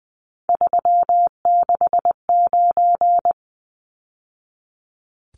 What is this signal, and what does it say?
Morse code, heard as a keyed tone, about 20 wpm, 700 Hz. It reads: 369
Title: Numbers - 3-Digit 20wpm